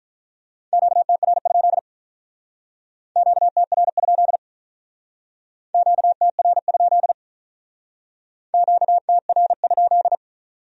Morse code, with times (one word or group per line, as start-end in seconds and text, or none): 0.7-1.8 QTR?
3.2-4.4 QTR?
5.7-7.1 QTR?
8.5-10.2 QTR?